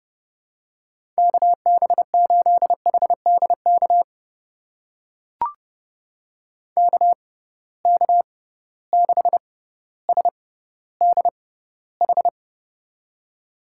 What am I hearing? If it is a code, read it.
Morse code, heard as a keyed tone, about 30 wpm, spaced as for 15 wpm, 700 Hz. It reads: KB8HDK KK6SDH
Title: USA Call Signs - 6x Character Spacing - 30wpm